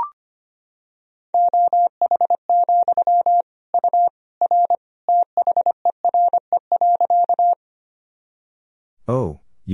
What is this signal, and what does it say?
Morse code, heard as a keyed tone, about 25 wpm, 700 Hz. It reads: OH, U R THERE.
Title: Sentences from Top 500 Words and Abbreviations 25wpm